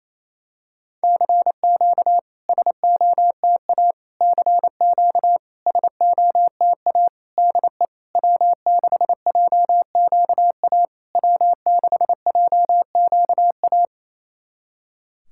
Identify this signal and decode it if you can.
Morse code, heard as a keyed tone, about 28 wpm, 700 Hz. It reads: CQ SOTA CQ SOTA DE W6JQA W6JQA
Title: SOTA QSO Practice 28wpm